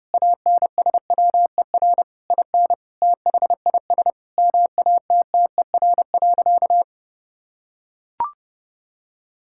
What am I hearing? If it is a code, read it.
Morse code, heard as a keyed tone, about 32 wpm, 700 Hz. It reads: ANSWER IN THIS MATTER.
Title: Sentences from Top 1600 Words 30wpm